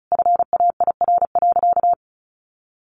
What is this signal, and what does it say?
Morse code, heard as a keyed tone, about 34 wpm, 700 Hz. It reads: FAIR.
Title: Sentences from Top 1300 Words 35wpm